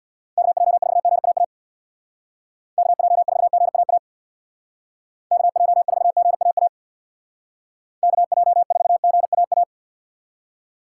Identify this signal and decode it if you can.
Morse code, heard as a keyed tone, about 66 wpm, 700 Hz. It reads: KJ4GAA KJ4GAA KJ4GAA KJ4GAA